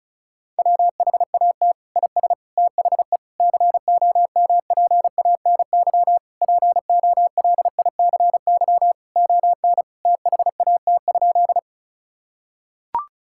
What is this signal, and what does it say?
Morse code, heard as a keyed tone, about 36 wpm, 700 Hz. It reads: WHAT IS THE COMPANY POLICY ON THAT?
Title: Sentences from Top 1000 Words - Review Part 5 - 35wpm